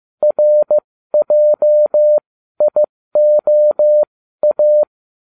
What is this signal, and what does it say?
Morse code, heard as a keyed tone, about 15 wpm, spaced as for 12 wpm, 600 Hz. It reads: RJIOA